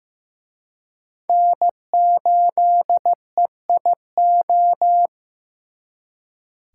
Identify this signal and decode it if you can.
Morse code, heard as a keyed tone, about 18 wpm, 700 Hz. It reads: N8EIO
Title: Speed Racing - USA 1x3 Call Signs 15wpm